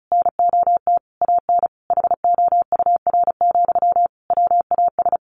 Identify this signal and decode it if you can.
Morse code, heard as a keyed tone, about 34 wpm, 700 Hz. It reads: NOT AN HOUR, WAS
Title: Sentences from Top 1400 Words 35wpm